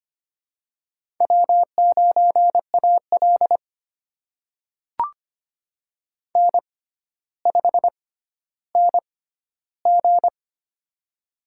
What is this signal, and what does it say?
Morse code, heard as a keyed tone, about 25 wpm, spaced as for 13 wpm, 700 Hz. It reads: W9AL N5NG
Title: USA Call Signs - 4 Characters - 6x Character Spacing - 25wpm